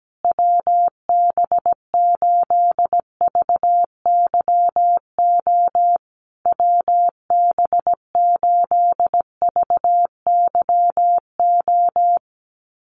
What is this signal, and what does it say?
Morse code, heard as a keyed tone, about 17 wpm, 700 Hz. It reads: WB8VYO WB8VYO
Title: USA Call Signs Repeated 3x 17wpm